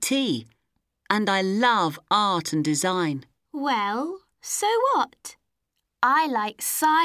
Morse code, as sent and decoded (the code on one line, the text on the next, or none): none